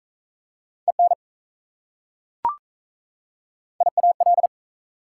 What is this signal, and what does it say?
Morse code, heard as a keyed tone, about 38 wpm, 690 Hz. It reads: EN IAL